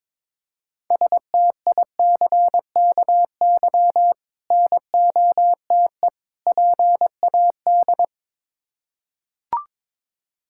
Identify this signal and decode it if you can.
Morse code, heard as a keyed tone, about 21 wpm, 690 Hz. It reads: STICKY NOTE PAD
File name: Difficult Words with Hints - 22wpm